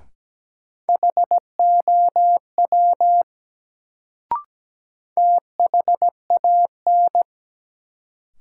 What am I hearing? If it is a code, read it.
Morse code, heard as a keyed tone, about 17 wpm, 700 Hz. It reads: HOW THAN
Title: Sentences from Top 100 Words - New Word Drill - 17wpm